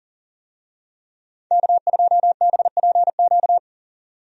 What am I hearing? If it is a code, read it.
Morse code, heard as a keyed tone, about 40 wpm, 700 Hz. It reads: K2BPQ
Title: USA Call Signs - 5 Characters - 40wpm